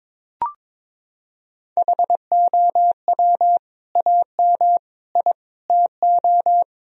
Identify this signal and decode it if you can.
Morse code, heard as a keyed tone, about 22 wpm, 700 Hz. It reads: HOW AM I TO
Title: Sentences from Top 1000 Words - Review Part 4 - 22wpm